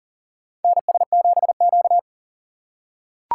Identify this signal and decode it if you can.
Morse code, heard as a keyed tone, about 40 wpm, 700 Hz. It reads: NS7Q
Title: USA Call Signs - 2x Character Spacing - 40wpm